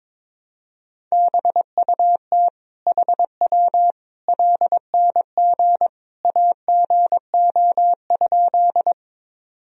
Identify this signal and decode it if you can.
Morse code, heard as a keyed tone, about 22 wpm, 700 Hz. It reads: BUT HW LNG AGO?
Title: Sentences from Top 500 Words and Abbreviations 22wpm